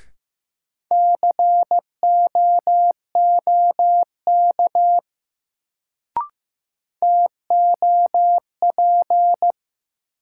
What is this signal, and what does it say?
Morse code, heard as a keyed tone, about 15 wpm, 700 Hz. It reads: COOK TOP
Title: Sentences from Top 700 Words - New Word Drill - 15wpm